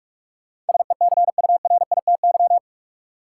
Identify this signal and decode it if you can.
Morse code, heard as a keyed tone, about 41 wpm, 700 Hz. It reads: SECURITY
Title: Sentences from Top 1400 Words - New Word Drill Alt - 45wpm